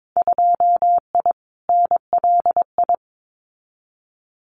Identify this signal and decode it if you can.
Morse code, heard as a keyed tone, about 22 wpm, 700 Hz. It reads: 2I NLI
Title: Winter Field Day Exchanges 22wpm